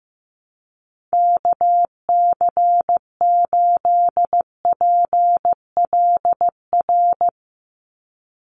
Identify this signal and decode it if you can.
Morse code, heard as a keyed tone, about 15 wpm, 700 Hz. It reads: KC8PLR
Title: USA Call Signs 15wpm